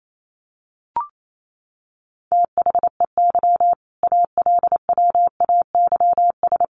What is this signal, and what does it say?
Morse code, heard as a keyed tone, about 28 wpm, 700 Hz. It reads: THEY ALWAYS
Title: Sentences from Top 300 Words 28wpm